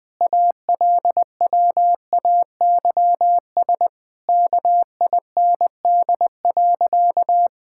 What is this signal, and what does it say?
Morse code, heard as a keyed tone, about 21 wpm, 700 Hz. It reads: ALWAYS KIND.
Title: Sentences with 2-Word Phrases - 20wpm